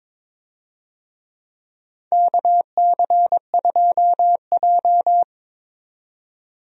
Morse code, heard as a keyed tone, about 22 wpm, 700 Hz. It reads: KC2J
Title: USA Call Signs - 4 Characters - 22wpm